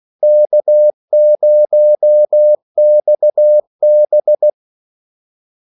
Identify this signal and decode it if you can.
Morse code, heard as a keyed tone, about 16 wpm, 600 Hz. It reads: K0XB
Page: Kaldesignaler 16 wpm